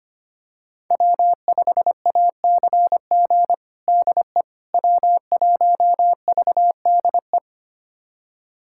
Morse code, heard as a keyed tone, about 25 wpm, 700 Hz. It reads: W5ACG DE W1VDE
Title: USA Callsign DE Callsign - 5 Characters - 25wpm